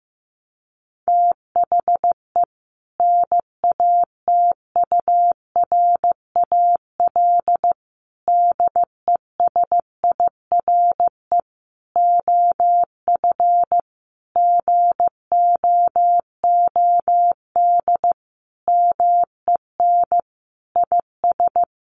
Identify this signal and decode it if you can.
Morse code, heard as a keyed tone, about 15 wpm, 700 Hz. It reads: THE NATURAL DESIRE OF GOOD MEN IS
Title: Sentences from Top 1100 Words 15wpm